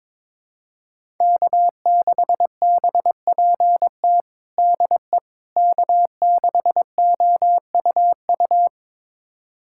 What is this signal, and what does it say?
Morse code, heard as a keyed tone, about 22 wpm, 700 Hz. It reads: K6BPT DE K6OUU
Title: USA Callsign DE Callsign - 5 Characters - 22wpm